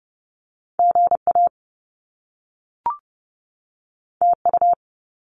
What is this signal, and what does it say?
Morse code, heard as a keyed tone, about 30 wpm, 700 Hz. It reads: G A T U
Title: Top Two Letter Combinations in Words 30wpm